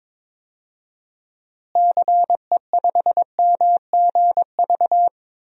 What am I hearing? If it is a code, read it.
Morse code, heard as a keyed tone, about 22 wpm, 700 Hz. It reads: CE5MGV
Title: DXCC-100 Call Signs - 6 Characters - Repeated 3x - 22wpm